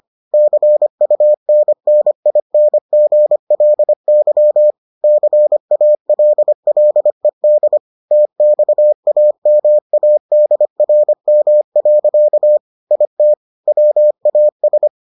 Morse code, heard as a keyed tone, about 25 wpm, 600 Hz. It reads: CUNNINGLY CALLED TXAMADRM. IT WAS